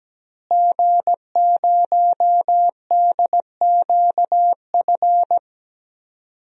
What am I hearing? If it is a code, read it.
Morse code, heard as a keyed tone, about 17 wpm, 700 Hz. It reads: G0DQF
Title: DXCC-100 Call Signs - 5 Characters - Repeated 3x - 17wpm